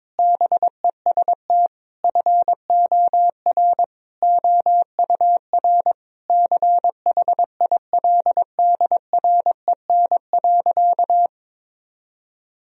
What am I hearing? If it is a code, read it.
Morse code, heard as a keyed tone, about 22 wpm, 700 Hz. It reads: BEST FOR OUR CHILDREN.
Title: Sentences from Top 1000 Words - Review Part 4 - 22wpm